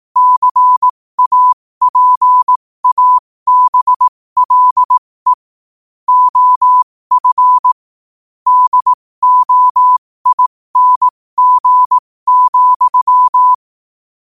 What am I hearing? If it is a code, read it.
Morse code, heard as a keyed tone, about 18 wpm, spaced as for 15 wpm, 1000 Hz. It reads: CAPABLE OF DOING,
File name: qotdfiles / 15wpm